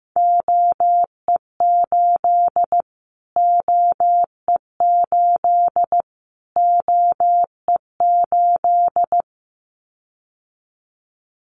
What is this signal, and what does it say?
Morse code, heard as a keyed tone, about 15 wpm, 700 Hz. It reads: OE8 OE8 OE8
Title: Callsign Prefixes for Top 25 DXCC - Repeated 3x - 15wpm